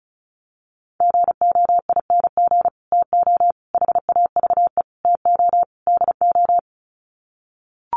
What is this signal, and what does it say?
Morse code, heard as a keyed tone, about 34 wpm, 700 Hz. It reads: GOING TO HAVE TO DO
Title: Sets of 5 Words Encore 35wpm